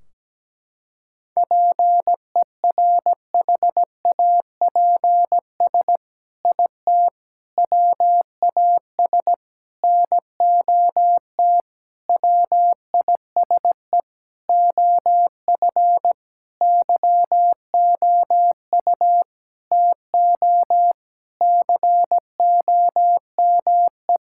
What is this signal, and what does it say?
Morse code, heard as a keyed tone, about 17 wpm, 700 Hz. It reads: PERHAPS IT WAS NOT WISE OF YOU TO COME
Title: Sentences from Top 1600 Words 17wpm